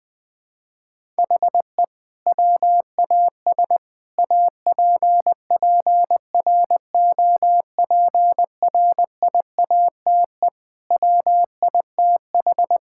Morse code, heard as a keyed tone, about 21 wpm, 700 Hz. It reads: HE WAS APPROPRIATE WITH
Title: Sentences from Top 1000 Words - Review Part 5 - 20wpm